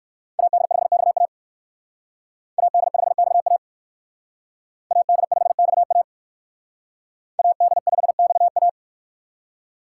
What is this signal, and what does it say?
Morse code, heard as a keyed tone, about 57 wpm, 700 Hz. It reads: AD5XA AD5XA AD5XA AD5XA